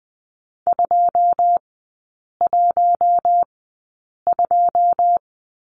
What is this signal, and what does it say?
Morse code, heard as a keyed tone, about 20 wpm, spaced as for 8 wpm, 700 Hz. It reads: 212
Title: 087 Lesson 15e - Number 5 - Three Characters 20wpm